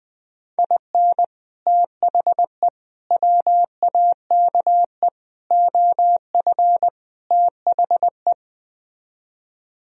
Morse code, heard as a keyed tone, about 20 wpm, 700 Hz. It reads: IN THE WAKE OF THE